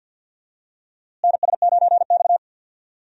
Morse code, heard as a keyed tone, about 53 wpm, 700 Hz. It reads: NS9X
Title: USA Call Signs - 4 Characters - Repeated 2x - 50wpm